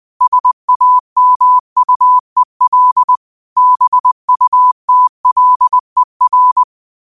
Morse code, heard as a keyed tone, about 20 wpm, 1000 Hz. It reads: SAMUEL BUTLER
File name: qotdfiles / 20wpm